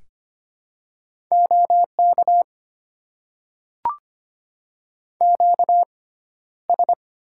Morse code, heard as a keyed tone, about 25 wpm, 700 Hz. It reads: O K Q S
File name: Top 90 QSO Elements - 6x Character Spacing - 25wpm